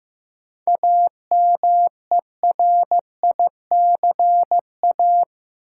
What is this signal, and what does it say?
Morse code, heard as a keyed tone, about 15 wpm, 700 Hz. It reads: AMERICA
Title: Sets of 5 Words - New Word Drill - 15wpm